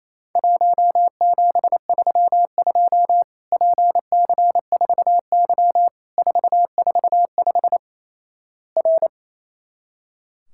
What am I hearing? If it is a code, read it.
Morse code, heard as a keyed tone, about 28 wpm, 700 Hz. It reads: 1732 PC4Y 445 R
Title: Worked All Europe DX Contest - QTC Exchange - 28wpm